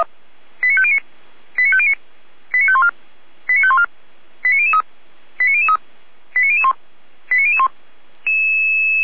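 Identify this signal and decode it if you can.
Five-tone selective call (ZVEI), heard as quick runs of five tones, tones of about 70 ms. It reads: 89509 89509 89514 89514 89003 89003 89001 89001
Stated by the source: ZVEI five-tone selective call received by SDR.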